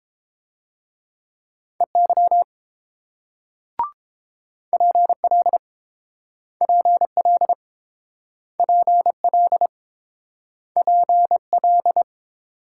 Morse code, heard as a keyed tone, about 32 wpm, 700 Hz. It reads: EY PL PL PL PL